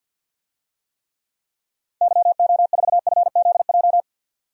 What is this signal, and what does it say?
Morse code, heard as a keyed tone, about 53 wpm, 700 Hz. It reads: YO4FZJ
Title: DXCC-100 Call Signs - 6 Characters - 50wpm